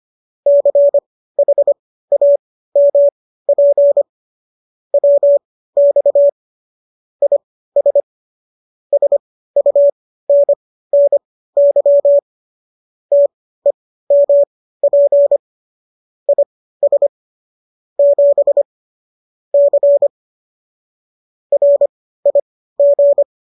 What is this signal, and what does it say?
Morse code, heard as a keyed tone, about 25 wpm, spaced as for 15 wpm, 570 Hz. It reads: CHAMP WX IS SUNNY TEMP IS 7 C RIG